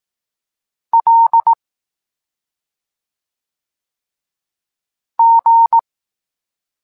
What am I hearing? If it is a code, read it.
Morse code, heard as a keyed tone, about 18 wpm, 920 Hz. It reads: L G
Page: Tekst 23.3 – Ciąg dalszy doskonalenia odbioru poznanych liter